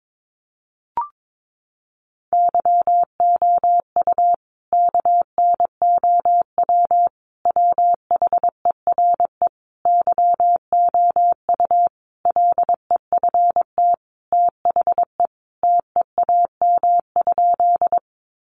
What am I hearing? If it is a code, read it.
Morse code, heard as a keyed tone, about 22 wpm, 700 Hz. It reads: YOU KNOW WHERE YOU LEFT THE TEAM?
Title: Sentences from Top 1500 Words 22wpm